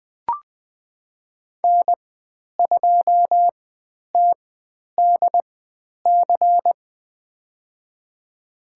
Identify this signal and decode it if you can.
Morse code, heard as a keyed tone, about 20 wpm, spaced as for 10 wpm, 700 Hz. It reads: N2TDC
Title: 095 Lesson 16g - Letter c - Call Signs 20wpm